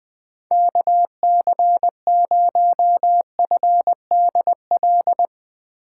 Morse code, heard as a keyed tone, about 20 wpm, 700 Hz. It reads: KC0FDL
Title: USA Call Signs - 6 Characters - 4x Character Spacing - 20wpm